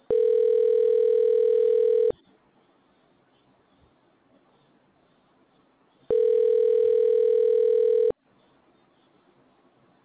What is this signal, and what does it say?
Ringback tone (North America), 440 + 480 Hz.